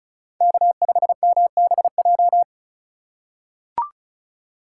Morse code, heard as a keyed tone, about 34 wpm, 700 Hz. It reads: K5MBJ
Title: USA Call Signs - 5 Characters - 4x Character Spacing - 35wpm